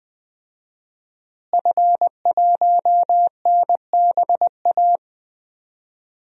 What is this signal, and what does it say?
Morse code, heard as a keyed tone, about 20 wpm, 700 Hz. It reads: F1NBA
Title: DXCC-100 Call Signs - 5 Characters - Repeated 2x - 20wpm